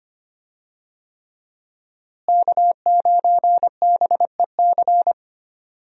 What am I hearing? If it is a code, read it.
Morse code, heard as a keyed tone, about 25 wpm, 700 Hz. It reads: K9BEC
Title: USA Call Signs - 5 Characters - Repeated 3x - 25wpm